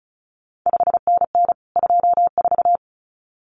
Morse code, heard as a keyed tone, about 36 wpm, 700 Hz. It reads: 5NN 24
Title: CQ WW DX Contest Exchange Encore - 35wpm